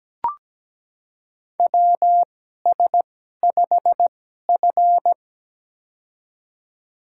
Morse code, heard as a keyed tone, about 17 wpm, spaced as for 12 wpm, 700 Hz. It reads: WS5F
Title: USA Call Signs - 4 Characters - 2x Character Spacing - 17wpm